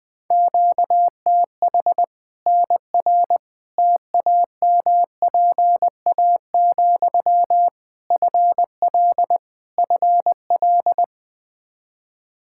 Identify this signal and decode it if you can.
Morse code, heard as a keyed tone, about 20 wpm, 700 Hz. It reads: QTH NR TAMPA, FL FL
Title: K1USN SST QSO Practice 20wpm